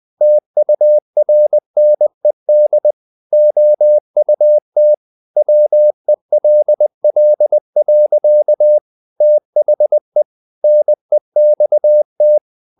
Morse code, heard as a keyed tone, about 20 wpm, 600 Hz. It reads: TURNED OUT WELL. THE NEXT
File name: the old man and the sea part 2 20wpm